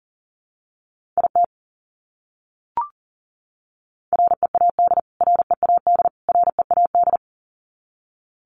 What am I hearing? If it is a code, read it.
Morse code, heard as a keyed tone, about 40 wpm, 700 Hz. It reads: IT READ READ READ
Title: Top 200 Words Repeated 40wpm